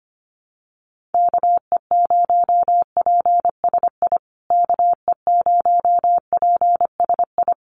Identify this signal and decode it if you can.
Morse code, heard as a keyed tone, about 25 wpm, 700 Hz. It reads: KE0PSI KE0PSI
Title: USA Call Signs Repeated 2x 25wpm